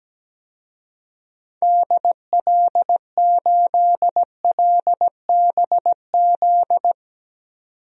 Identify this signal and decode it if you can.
Morse code, heard as a keyed tone, about 17 wpm, 700 Hz. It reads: DL8LBZ
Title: DXCC-100 Call Signs - 6 Characters - 17wpm